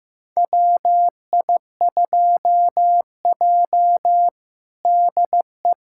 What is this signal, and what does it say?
Morse code, heard as a keyed tone, about 15 wpm, 700 Hz. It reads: WI2J DE
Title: USA Callsign DE Callsign - 4 Characters - 15wpm